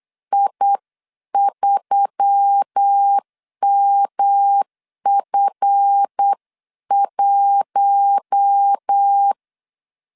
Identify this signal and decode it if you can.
Morse code, heard as a keyed tone, about 8 wpm, 800 Hz. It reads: I3MF1
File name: CW / DET / 1 9 1500C S C 8wpm 5653